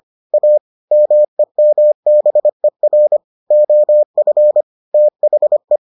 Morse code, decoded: A MEMBER OF THE

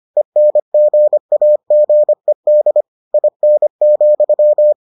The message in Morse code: ENGAGED IN,

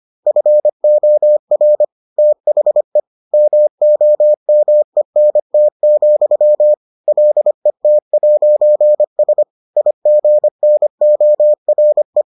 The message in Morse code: FOR THE MOMENT, LET'S IGNORE

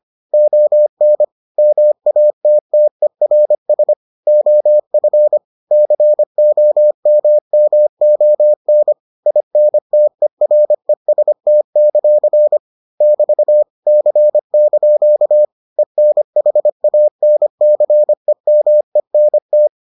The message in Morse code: ON MATTERS OF COMMON INTEREST; <BT> C) ENHANCEMENT

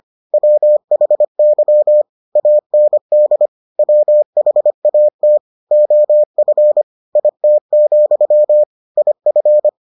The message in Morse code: WHY AND WHAT OF IT, IF

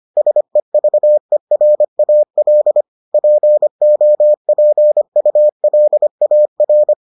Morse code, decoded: SEVERAL POPULAR

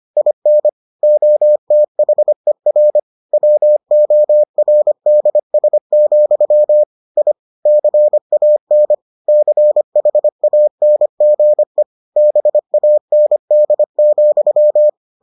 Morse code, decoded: IN OTHER WORDS, I CAN CHANGE BAND,

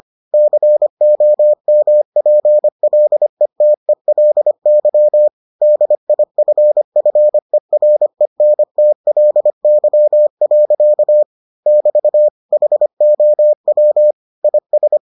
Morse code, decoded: COMPLETELY DIFFERENTLY. <BT> HOW IS

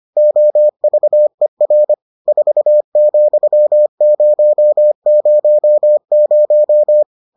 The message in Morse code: OVER 4,000

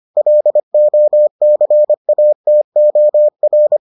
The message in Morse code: LOCATOR

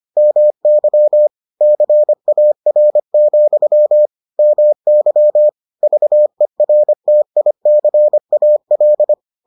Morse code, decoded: MY CAR, MY VERTICAL

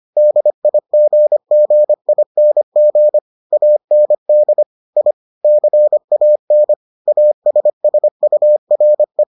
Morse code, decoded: DIGGING AND I CAN ASSURE

